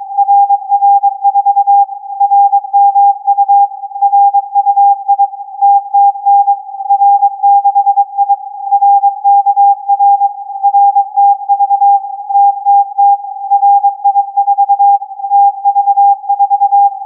Morse code, heard as a keyed tone, about 22 wpm, 800 Hz. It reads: RR4 RMU RUI TTN R6I RKR RTV TTT RI4 TV4
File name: CAS4B-TURN-AIRSPY-01